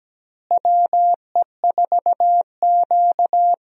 Morse code, heard as a keyed tone, about 17 wpm, 700 Hz. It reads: WE4Q
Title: USA Call Signs - 4 Characters - Repeated 3x - 17wpm